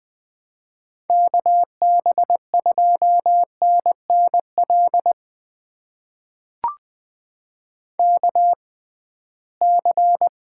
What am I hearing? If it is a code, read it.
Morse code, heard as a keyed tone, about 20 wpm, spaced as for 15 wpm, 700 Hz. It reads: KB2NNL KC
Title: USA Call Signs - 6 Characters - 6x Character Spacing - 20wpm